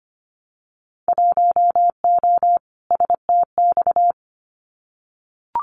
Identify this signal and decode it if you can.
Morse code, heard as a keyed tone, about 25 wpm, 700 Hz. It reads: 1O STX
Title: Winter Field Day Exchanges 25wpm